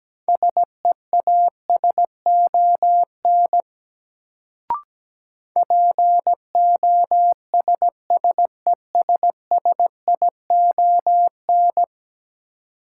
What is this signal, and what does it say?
Morse code, heard as a keyed tone, about 17 wpm, 700 Hz. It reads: SEASON POSSESSION
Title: Sentences from Top 1400 Words - New Word Drill - 17wpm